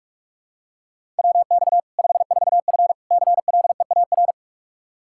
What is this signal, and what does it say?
Morse code, heard as a keyed tone, about 42 wpm, 700 Hz. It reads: WX 54F CLEAR
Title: K1USN SST QSO Practice 45wpm